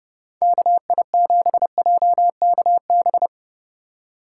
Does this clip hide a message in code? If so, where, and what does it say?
Yes: Morse code from 0.4 to 3.3 s, heard as a keyed tone, about 31 wpm, 700 Hz. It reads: KI7JKB